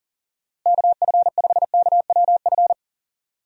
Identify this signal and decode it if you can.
Morse code, heard as a keyed tone, about 40 wpm, 700 Hz. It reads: KF5KWF
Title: USA Call Signs - 6 Characters - Repeated 2x - 40wpm